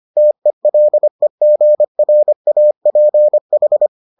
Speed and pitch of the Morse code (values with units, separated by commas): 25 wpm, 600 Hz